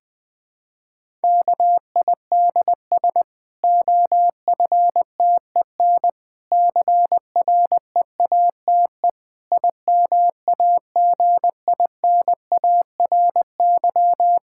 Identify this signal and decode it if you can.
Morse code, heard as a keyed tone, about 21 wpm, 700 Hz. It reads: KIDS OFTEN CREATE IMAGINARY
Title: Difficult Sentences with Hints - 20wpm